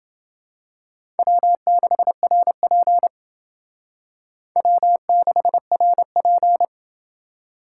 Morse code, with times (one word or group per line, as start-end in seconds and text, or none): 1.2-3.1 W6RP
4.6-6.7 W6RP